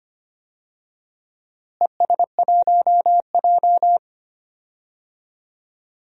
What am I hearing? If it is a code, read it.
Morse code, heard as a keyed tone, about 25 wpm, 700 Hz. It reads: ES1J
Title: DXCC-100 Call Signs - 4 Characters - Repeated 3x - 25wpm